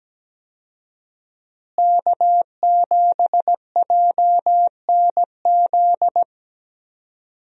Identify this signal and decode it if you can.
Morse code, heard as a keyed tone, about 17 wpm, 700 Hz. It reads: K7JNZ
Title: USA Call Signs - 5 Characters - Repeated 3x - 17wpm